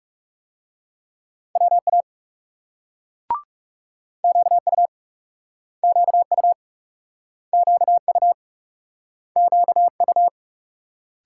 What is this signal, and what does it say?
Morse code, heard as a keyed tone, about 41 wpm, 700 Hz. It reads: WA QU QU QU QU